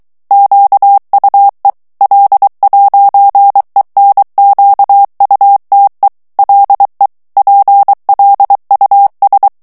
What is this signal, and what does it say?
Morse code, heard as a keyed tone, about 23 wpm, 800 Hz. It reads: QUE L'ENQUTE LE PLUS
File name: morse mp3 / FRANCAIS TEXTE CLAIR 1 VITESSE 22 MOTS MINUTES